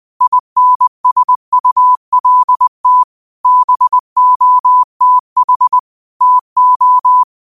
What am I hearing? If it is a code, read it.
Morse code, heard as a keyed tone, about 20 wpm, 1000 Hz. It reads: INSULT BOTH TO
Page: Quotes for Sat, 16 Aug 2025 in Morse Code at 20 words per minute.